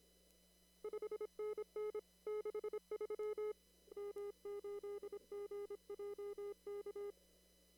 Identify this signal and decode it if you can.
Morse code, heard as a keyed tone, about 26 wpm, 420 Hz. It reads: 5NN 63 M8GJK
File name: CW / kt5j wpx cw-op-k5pi